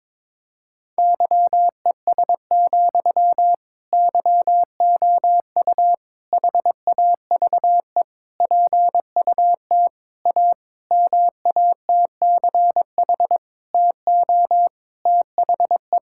Morse code, heard as a keyed tone, about 22 wpm, 700 Hz. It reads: YES, YOU HAVE PUT A MATCH TO THE
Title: Sentences from Top 1600 Words 22wpm